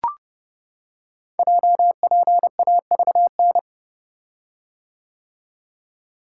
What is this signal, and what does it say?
Morse code, heard as a keyed tone, about 28 wpm, 700 Hz. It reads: JPAVN
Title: Code Groups - 5 Letters - 30wpm